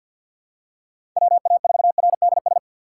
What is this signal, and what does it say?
Morse code, heard as a keyed tone, about 48 wpm, 700 Hz. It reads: WA4RDS
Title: USA Call Signs - 6 Characters - 6x Character Spacing - 50wpm